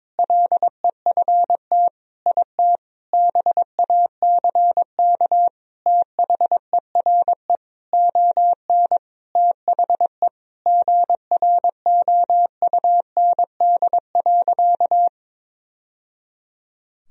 Morse code, heard as a keyed tone, about 22 wpm, 700 Hz. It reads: LEFT IT BACK THERE ON THE GROUND.
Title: Sentences from Top 1000 Words - Review Part 2 - 22wpm